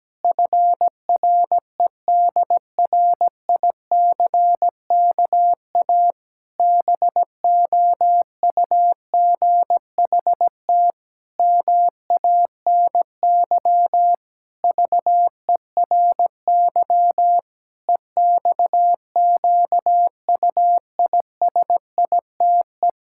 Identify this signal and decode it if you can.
Morse code, heard as a keyed tone, about 17 wpm, 700 Hz. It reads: FREDRICKA BOUGHT MANY VERY EXQUISITE
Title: Pangram Challenge v2 - 17wpm